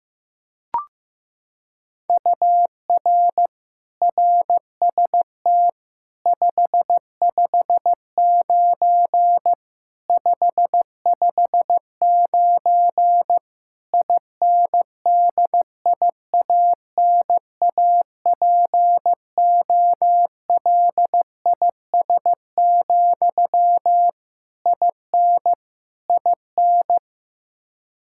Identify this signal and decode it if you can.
Morse code, heard as a keyed tone, about 15 wpm, 700 Hz. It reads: UR RST 559 559 INDIANAPOLIS, IN IN
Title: Practice QSO - Exchange 1 - Beginning - 15wpm